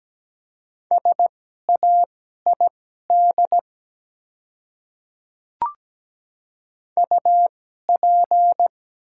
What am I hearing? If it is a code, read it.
Morse code, heard as a keyed tone, about 17 wpm, spaced as for 12 wpm, 700 Hz. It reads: SAID UP
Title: 2x Character Spacing - Top 200 Words - 17wpm